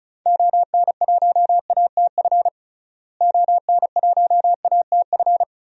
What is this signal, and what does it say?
Morse code, heard as a keyed tone, about 36 wpm, 700 Hz. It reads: ON1ATF ON1ATF
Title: DXCC-100 Call Signs - 6 Characters - Repeated 3x - 35wpm